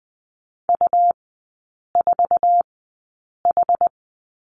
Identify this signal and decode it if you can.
Morse code, heard as a keyed tone, about 20 wpm, spaced as for 8 wpm, 700 Hz. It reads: U4H
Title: 205 Lesson 33e - Letter j - Three Characters 20wpm